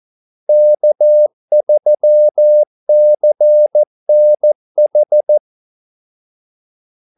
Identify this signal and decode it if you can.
Morse code, heard as a keyed tone, about 14 wpm, 600 Hz. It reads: K3CNH
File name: Callsigns-14wpm